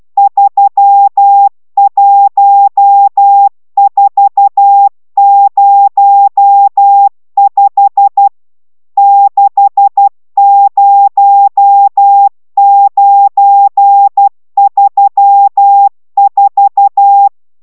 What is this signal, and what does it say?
Morse code, heard as a keyed tone, about 12 wpm, 800 Hz. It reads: 31405 60934